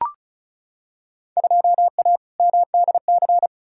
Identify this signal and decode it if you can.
Morse code, heard as a keyed tone, about 34 wpm, 700 Hz. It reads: 2A MDC
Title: Field Day Exchanges - 35wpm